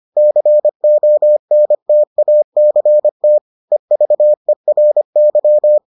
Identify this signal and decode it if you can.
Morse code, heard as a keyed tone, about 25 wpm, 600 Hz. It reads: CONTACT EVERY